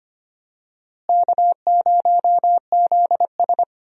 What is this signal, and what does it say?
Morse code, heard as a keyed tone, about 25 wpm, 700 Hz. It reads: K0ZS
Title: USA Call Signs - 4 Characters - Repeated 2x - 25wpm